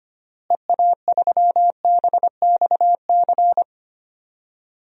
Morse code, heard as a keyed tone, about 25 wpm, 700 Hz. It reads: EA3BXC
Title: DXCC-100 Call Signs - 6 Characters - Repeated 3x - 25wpm